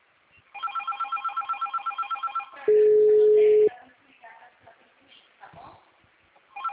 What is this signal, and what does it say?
Ringback tone (Europe, 425 Hz), with speech.